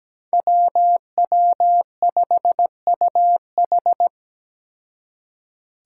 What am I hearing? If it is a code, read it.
Morse code, heard as a keyed tone, about 17 wpm, 700 Hz. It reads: WW5UH
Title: Five Challenging Letter-Number Combinations 17wpm